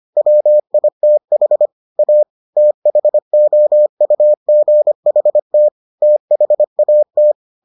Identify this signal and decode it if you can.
Morse code, heard as a keyed tone, about 25 wpm, 600 Hz. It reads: WITH A THOUGHT THAT